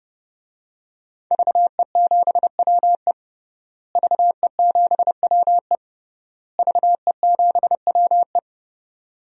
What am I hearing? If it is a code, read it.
Morse code, heard as a keyed tone, about 30 wpm, 700 Hz. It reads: VE7WE VE7WE VE7WE
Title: DXCC-100 Call Signs - 5 Characters - Repeated 3x - 30wpm